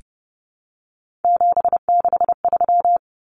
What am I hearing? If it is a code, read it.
Morse code, heard as a keyed tone, about 30 wpm, 700 Hz. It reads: 763
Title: Numbers - 3-Digit 30wpm